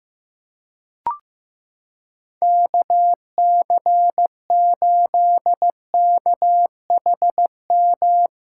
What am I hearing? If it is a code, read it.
Morse code, heard as a keyed tone, about 15 wpm, 700 Hz. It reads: KC8KHM
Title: USA Callsign DE Callsign - 6 Characters - 15wpm